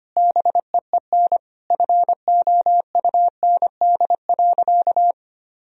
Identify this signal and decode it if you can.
Morse code, heard as a keyed tone, about 25 wpm, 700 Hz. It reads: BEEN FOUND.
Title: Sentences from Top 1000 Words - Review Part 4 - 25wpm